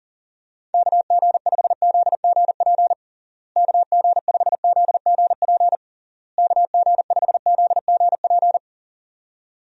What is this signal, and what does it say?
Morse code, heard as a keyed tone, about 40 wpm, 700 Hz. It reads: KG5ZGP KG5ZGP KG5ZGP
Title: USA Call Signs - 6 Characters - Repeated 3x - 40wpm